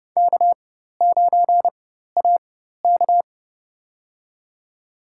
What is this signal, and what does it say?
Morse code, heard as a keyed tone, about 30 wpm, spaced as for 14 wpm, 700 Hz. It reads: K9AK
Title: USA Call Signs - 4 Characters - 4x Character Spacing - 30wpm